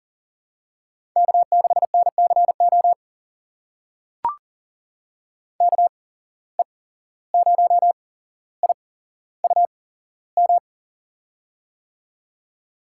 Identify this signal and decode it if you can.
Morse code, heard as a keyed tone, about 38 wpm, spaced as for 16 wpm, 700 Hz. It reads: K6NCO KE0IUM
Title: USA Call Signs - 8x Character Spacing - 40wpm